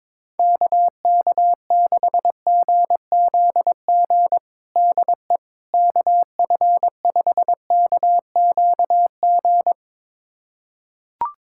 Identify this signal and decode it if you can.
Morse code, heard as a keyed tone, about 22 wpm, 700 Hz. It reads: KK6GZG DE KF5KQG
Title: USA Callsign DE Callsign - 6 Characters - 22wpm